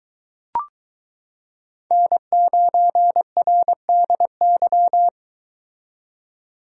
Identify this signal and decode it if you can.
Morse code, heard as a keyed tone, about 22 wpm, 700 Hz. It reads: N9RDY